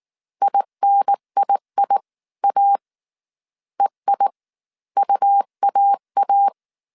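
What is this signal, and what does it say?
Morse code, heard as a keyed tone, about 19 wpm, spaced as for 15 wpm, 800 Hz. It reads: INIIA EIUAA
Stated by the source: Morse code dictation at 18 wpm from the IW3SGT QRS Morse code training course, five-character groups of 300 characters.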